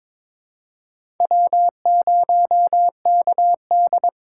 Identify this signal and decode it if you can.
Morse code, heard as a keyed tone, about 22 wpm, 700 Hz. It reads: W0KD
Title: USA Call Signs - 4 Characters - 3x Character Spacing - 22wpm